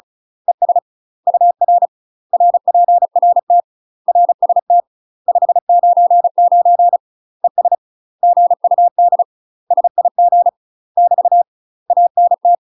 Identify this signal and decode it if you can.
Morse code, heard as a keyed tone, about 35 wpm, 700 Hz. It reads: ES UR RPRT RST 599 ES GUD SIG <BT> ANT
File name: UK MASTER QSOs TRAINING / 35WPM / UK MASTER QSOs 0017